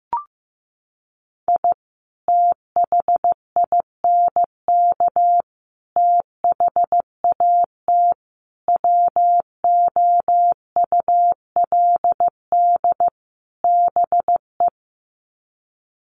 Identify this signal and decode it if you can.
Morse code, heard as a keyed tone, about 15 wpm, 700 Hz. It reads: I THINK THAT WOULD BE
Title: Sets of 5 Words Encore2 15wpm